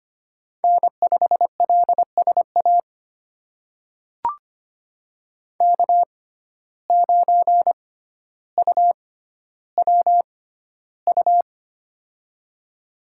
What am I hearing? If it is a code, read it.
Morse code, heard as a keyed tone, about 25 wpm, spaced as for 13 wpm, 700 Hz. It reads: N5LSA K9UWU
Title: USA Call Signs - 5 Characters - 6x Character Spacing - 25wpm